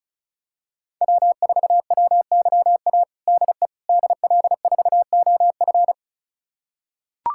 Morse code, heard as a keyed tone, about 34 wpm, 700 Hz. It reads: W4WYA DE DL4OF
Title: USA Callsign DE Callsign - 5 Characters - 35wpm